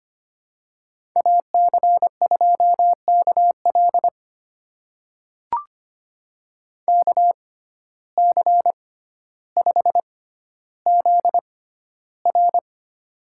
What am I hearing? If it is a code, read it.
Morse code, heard as a keyed tone, about 25 wpm, spaced as for 13 wpm, 700 Hz. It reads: AC2KL KC5ZR
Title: USA Call Signs - 5 Characters - 6x Character Spacing - 25wpm